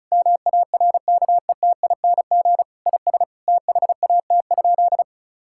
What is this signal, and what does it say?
Morse code, heard as a keyed tone, about 36 wpm, 700 Hz. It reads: MARKETING IS THAT?
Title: Sentences from Top 1000 Words - Review Part 5 - 35wpm